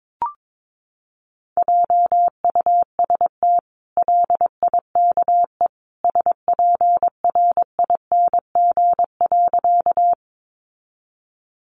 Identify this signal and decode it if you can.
Morse code, heard as a keyed tone, about 22 wpm, 700 Hz. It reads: JUST LIKE SPRING.
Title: Sentences from Top 1200 Words 22wpm